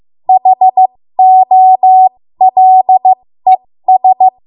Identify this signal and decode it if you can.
Morse code, heard as a keyed tone, about 15 wpm, spaced as for 13 wpm, 750 Hz. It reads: HOLES
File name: Morse / 13WPM